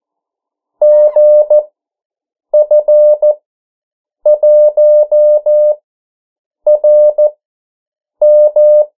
Morse code, heard as a keyed tone, about 14 wpm, spaced as for 7 wpm, 600 Hz. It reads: GF1RM
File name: morse mp3 / 09 urok 8